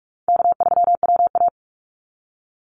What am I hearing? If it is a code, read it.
Morse code, heard as a keyed tone, about 46 wpm, 700 Hz. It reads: K3WA
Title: Easy Call Signs - 45wpm